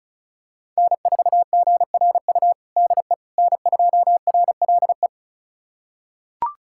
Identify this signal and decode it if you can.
Morse code, heard as a keyed tone, about 36 wpm, 690 Hz. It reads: N4GRU DE N2RLE
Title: USA Callsign DE Callsign - 5 Characters - 35wpm